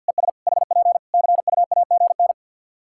Morse code, heard as a keyed tone, about 51 wpm, 700 Hz. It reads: ES HP CUAGN
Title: Practice QSO - Exchange 4 - Ending - 50wpm